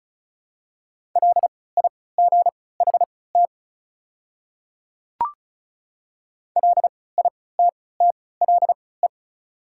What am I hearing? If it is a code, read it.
Morse code, heard as a keyed tone, about 36 wpm, spaced as for 20 wpm, 700 Hz. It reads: LIGHT LITTLE
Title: 3x Character Spacing - Top 200 Words - 35wpm